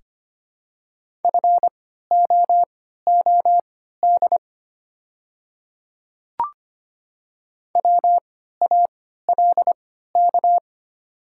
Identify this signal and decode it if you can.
Morse code, heard as a keyed tone, about 25 wpm, spaced as for 14 wpm, 690 Hz. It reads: FOOD WALK
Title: 3x Character Spacing - Top 300 Words - 25wpm